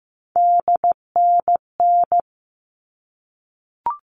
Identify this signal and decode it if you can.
Morse code, heard as a keyed tone, about 15 wpm, 700 Hz. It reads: DNN
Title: All Cut Numbers - 3-Digit 15wpm